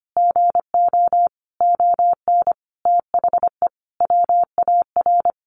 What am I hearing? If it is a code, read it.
Morse code, heard as a keyed tone, about 25 wpm, 700 Hz. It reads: GO ON THE WAR
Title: Sentences from Top 1400 Words 25wpm